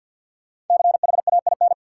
Morse code, heard as a keyed tone, about 52 wpm, 700 Hz. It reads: CHAIN
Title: Sentences from Top 1500 Words - New Word Drill - 50wpm